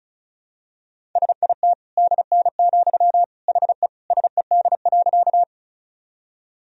Morse code, heard as a keyed tone, about 34 wpm, 700 Hz. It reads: SIT DN, HE SED.
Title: Sentences from Top 500 Words and Abbreviations 35wpm